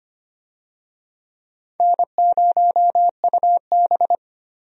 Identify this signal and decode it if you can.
Morse code, heard as a keyed tone, about 25 wpm, 700 Hz. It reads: N0UB
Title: USA Call Signs - 4 Characters - Repeated 2x - 25wpm